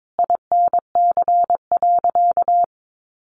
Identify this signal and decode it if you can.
Morse code, heard as a keyed tone, about 22 wpm, 700 Hz. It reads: INC.
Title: A Sound Of Thunder 22wpm